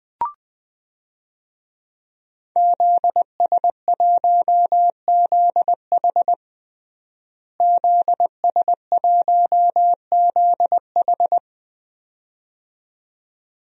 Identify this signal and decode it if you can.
Morse code, heard as a keyed tone, about 20 wpm, 700 Hz. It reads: ZS1ZH ZS1ZH
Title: DXCC-100 Call Signs - 5 Characters - Repeated 2x - 20wpm